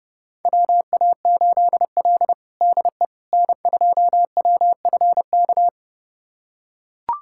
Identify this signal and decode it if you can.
Morse code, heard as a keyed tone, about 32 wpm, 700 Hz. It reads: WA8L DE N2WFK
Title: USA Callsign DE Callsign 30wpm